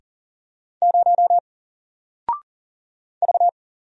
Morse code, heard as a keyed tone, about 40 wpm, 700 Hz. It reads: 0 V
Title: Single Character (Letter:Number:Punctuation:Prosign) 40wpm